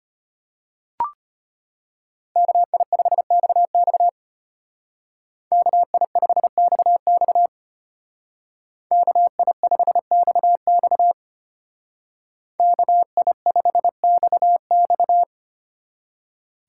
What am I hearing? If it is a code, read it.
Morse code, heard as a keyed tone, about 31 wpm, 700 Hz. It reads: KI5XX KI5XX KI5XX KI5XX
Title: Speed Racing - USA 2x2 Call Signs 25wpm